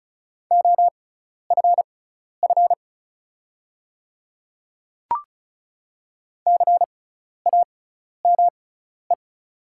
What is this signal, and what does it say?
Morse code, heard as a keyed tone, about 36 wpm, spaced as for 12 wpm, 690 Hz. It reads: OFF CAME
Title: 6x Character Spacing - Top 300 Words - 35wpm